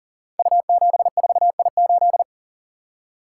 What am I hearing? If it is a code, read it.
Morse code, heard as a keyed tone, about 39 wpm, 700 Hz. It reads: U74I8
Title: Code Groups - 5 Letter:Numbers - 40wpm